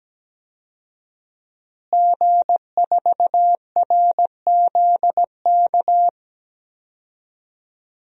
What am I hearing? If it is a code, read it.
Morse code, heard as a keyed tone, about 17 wpm, 700 Hz. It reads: G4RZK
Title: DXCC-100 Call Signs - 5 Characters - Repeated 3x - 17wpm